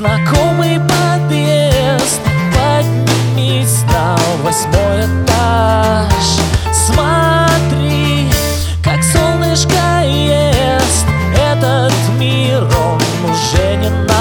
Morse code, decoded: none